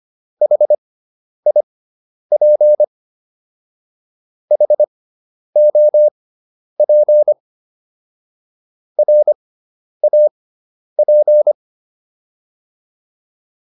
Morse code, heard as a keyed tone, about 25 wpm, spaced as for 10 wpm, 600 Hz. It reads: HIP HOP RAP